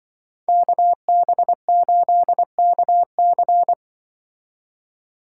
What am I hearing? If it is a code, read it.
Morse code, heard as a keyed tone, about 25 wpm, 700 Hz. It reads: KB8KC
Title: Speed Racing - USA 2x2 Call Signs 20wpm